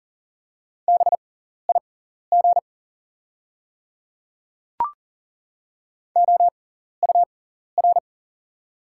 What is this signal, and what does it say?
Morse code, heard as a keyed tone, about 40 wpm, spaced as for 14 wpm, 700 Hz. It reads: BIG OUR
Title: 6x Character Spacing - Top 200 Words - 40wpm